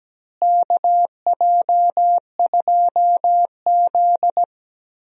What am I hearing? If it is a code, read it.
Morse code, heard as a keyed tone, about 17 wpm, 700 Hz. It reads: KJ2Z
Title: USA Call Signs - 4 Characters - Repeated 3x - 17wpm